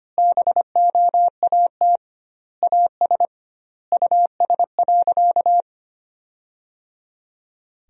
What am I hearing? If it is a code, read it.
Morse code, heard as a keyed tone, about 25 wpm, 700 Hz. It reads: BOAT AS US.
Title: 2x Word-Spacing - Sentences from Top 500 Words - 25wpm